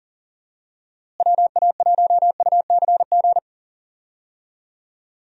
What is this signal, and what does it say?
Morse code, heard as a keyed tone, about 37 wpm, 700 Hz. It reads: WA1UCG
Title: USA Call Signs - 6 Characters - Repeated 2x - 40wpm